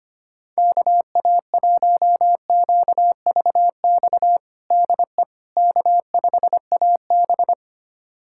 Morse code, heard as a keyed tone, about 25 wpm, 700 Hz. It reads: KA1QVX DE K5AB
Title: USA Callsign DE Callsign 25wpm